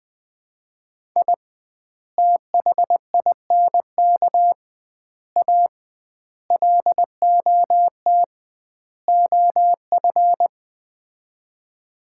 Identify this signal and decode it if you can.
Morse code, heard as a keyed tone, about 21 wpm, 700 Hz. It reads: I THINK A LOT OF
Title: 2x Word-Spacing - Sets of 5 Words - 20wpm